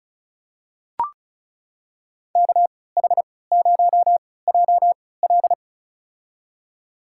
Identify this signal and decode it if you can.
Morse code, heard as a keyed tone, about 34 wpm, spaced as for 20 wpm, 700 Hz. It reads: KH0JL
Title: USA Call Signs - 5 Characters - 3x Character Spacing - 35wpm